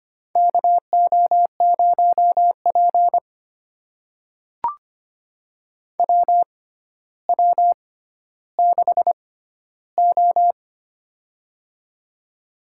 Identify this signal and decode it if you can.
Morse code, heard as a keyed tone, about 25 wpm, spaced as for 13 wpm, 700 Hz. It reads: KO0P WW6O
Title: USA Call Signs - 4 Characters - 6x Character Spacing - 25wpm